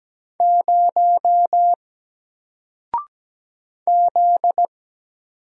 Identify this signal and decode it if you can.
Morse code, heard as a keyed tone, about 17 wpm, 700 Hz. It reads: 0 Z
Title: Single Letter-Number Alt 17wpm